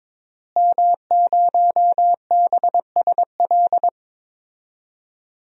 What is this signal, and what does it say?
Morse code, heard as a keyed tone, about 22 wpm, 700 Hz. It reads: M0BSL
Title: DXCC-100 Call Signs - 5 Characters - Repeated 2x - 22wpm